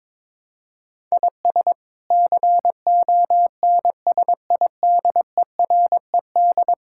Morse code, heard as a keyed tone, about 22 wpm, 700 Hz. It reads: IS CONSIDERED
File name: practice / 22wpm / Sets of 7 Words - 22wpm